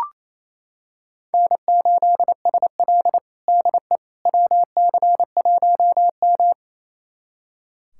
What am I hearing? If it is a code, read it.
Morse code, heard as a keyed tone, about 28 wpm, 700 Hz. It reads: N8SL DE WC1M
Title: USA Callsign DE Callsign - 4 Characters - 28wpm